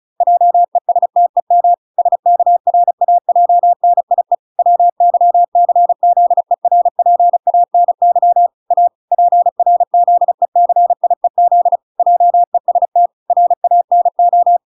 Morse code, heard as a keyed tone, about 36 wpm, 700 Hz. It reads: JESTEM SKRAJNIE WYCZERPANY A PRZECIEZ JEST RANO